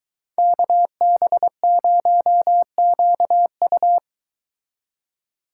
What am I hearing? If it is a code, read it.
Morse code, heard as a keyed tone, about 23 wpm, 700 Hz. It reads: KB0QU